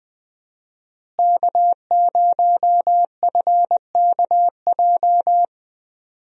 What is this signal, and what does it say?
Morse code, heard as a keyed tone, about 21 wpm, 700 Hz. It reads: K0FKJ
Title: USA Call Signs - 5 Characters - Repeated 2x - 20wpm